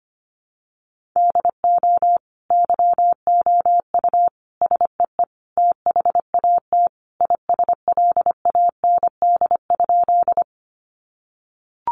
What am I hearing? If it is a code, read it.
Morse code, heard as a keyed tone, about 25 wpm, 700 Hz. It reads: DO YOU SEE THAT ISLAND?
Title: Sentences from Top 500 Words 25wpm